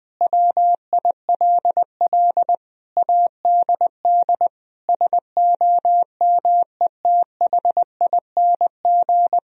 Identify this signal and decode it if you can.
Morse code, heard as a keyed tone, about 20 wpm, 700 Hz. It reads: WILL ADD SOMETHING
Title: Sentences from Top 1000 Words - Review Part 1 - 20wpm